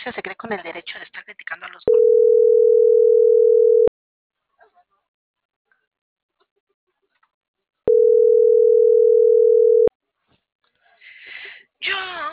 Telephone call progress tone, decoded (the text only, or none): ringback tone